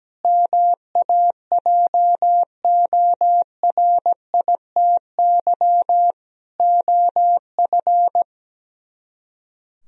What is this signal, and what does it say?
Morse code, heard as a keyed tone, about 17 wpm, 700 Hz. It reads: MAJORITY OF
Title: Speed-Racing - Sets of 3 Words 17wpm